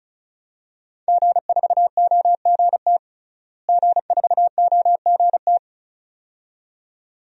Morse code, heard as a keyed tone, about 36 wpm, 700 Hz. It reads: G4OGT G4OGT
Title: DXCC-100 Call Signs - 5 Characters - Repeated 2x - 35wpm